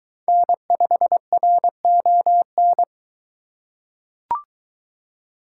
Morse code, heard as a keyed tone, about 22 wpm, 700 Hz. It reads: N5RON